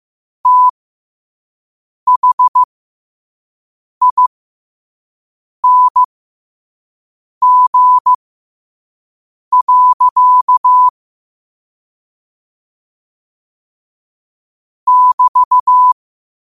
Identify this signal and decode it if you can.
Morse code, heard as a keyed tone, about 15 wpm, spaced as for 5 wpm, 1000 Hz. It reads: THING. <BT>
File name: qotdfiles / 5wpm